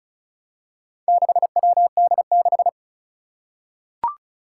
Morse code, heard as a keyed tone, about 36 wpm, 700 Hz. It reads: 6WD6
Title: Four Challenging Letter-Number Combinations 35wpm